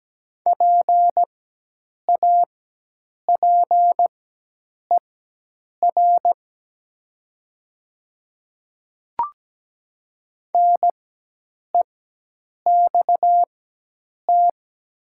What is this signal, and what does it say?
Morse code, heard as a keyed tone, about 17 wpm, spaced as for 8 wpm, 700 Hz. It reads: PAPER NEXT
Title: 4x Character Spacing - Top 300 Words - 17wpm